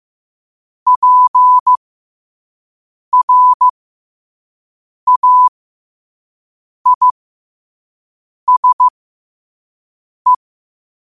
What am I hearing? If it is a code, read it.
Morse code, heard as a keyed tone, about 15 wpm, spaced as for 5 wpm, 1000 Hz. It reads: PRAISE
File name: qotdfiles / 5wpm